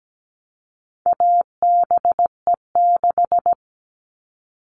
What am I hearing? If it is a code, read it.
Morse code, heard as a keyed tone, about 17 wpm, 700 Hz. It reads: ABE6
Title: All Cut Numbers - 4-Digit 17wpm